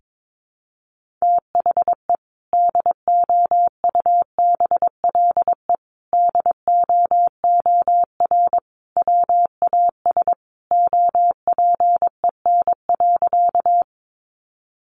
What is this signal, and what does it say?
Morse code, heard as a keyed tone, about 22 wpm, 700 Hz. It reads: THE DOUBLE DOOR WAS OPEN.
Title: Sentences from Top 1400 Words 22wpm